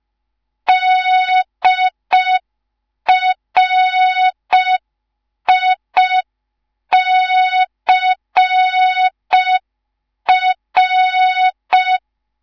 Morse code, heard as a keyed tone, about 5 wpm, 3700 Hz. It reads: DRICR
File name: Morze-HA2MN / hang / 10GWZ HA2MN